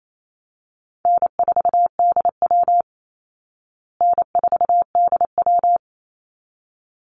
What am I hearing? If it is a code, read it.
Morse code, heard as a keyed tone, about 28 wpm, 700 Hz. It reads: N4DW N4DW
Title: Easy Call Signs Repeated 2x - 28wpm